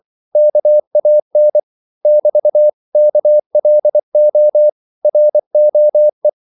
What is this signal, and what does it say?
Morse code, heard as a keyed tone, about 24 wpm, 600 Hz. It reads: KAN <BT> KLO ROE
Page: Korte danske ord 24wpm